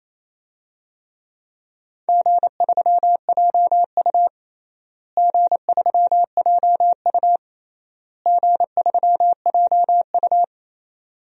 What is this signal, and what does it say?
Morse code, heard as a keyed tone, about 28 wpm, 700 Hz. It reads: G3JU G3JU G3JU
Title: DXCC-100 Call Signs - 4 Characters - Repeated 3x - 28wpm